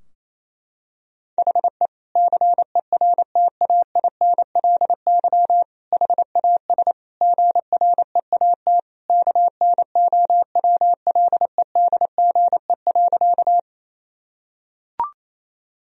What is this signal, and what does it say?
Morse code, heard as a keyed tone, about 28 wpm, 690 Hz. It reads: HE CERTAINLY HAS GREAT KNOWLEDGE.
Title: Sentences from Top 1000 Words - Review Part 3 - 28wpm